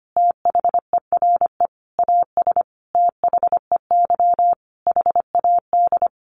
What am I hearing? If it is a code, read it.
Morse code, heard as a keyed tone, about 25 wpm, 700 Hz. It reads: THERE AS THEY HAD
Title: A Sound Of Thunder 25wpm